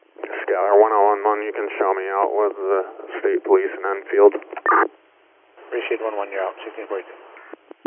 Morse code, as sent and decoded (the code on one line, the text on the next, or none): none